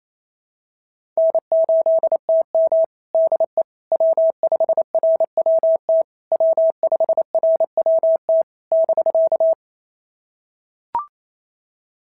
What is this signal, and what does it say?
Morse code, heard as a keyed tone, about 27 wpm, 660 Hz. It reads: N8TM DE W5RWT W5RWT <BK>
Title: SOTA QSO Practice 28wpm